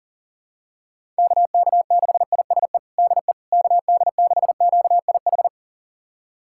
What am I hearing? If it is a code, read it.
Morse code, heard as a keyed tone, about 37 wpm, 700 Hz. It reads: KK6ISE DE KD6QIH
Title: USA Callsign DE Callsign - 6 Characters - 40wpm